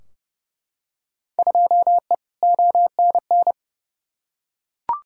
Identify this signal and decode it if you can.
Morse code, heard as a keyed tone, about 30 wpm, 700 Hz. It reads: 2E ONN
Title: Field Day Exchanges Encore - 30wpm